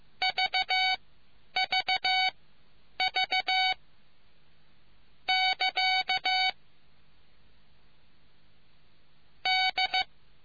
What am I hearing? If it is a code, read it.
Morse code, heard as a keyed tone, about 15 wpm, spaced as for 9 wpm, 2250 Hz. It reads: VVV <KA> D